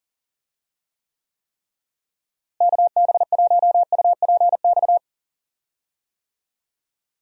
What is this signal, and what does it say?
Morse code, heard as a keyed tone, about 40 wpm, 700 Hz. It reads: KB1UPX
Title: USA Call Signs - 6 Characters - Repeated 3x - 40wpm